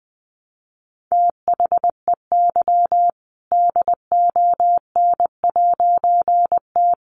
Morse code, heard as a keyed tone, about 20 wpm, 700 Hz. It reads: THEY DON'T
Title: A Sound Of Thunder 20wpm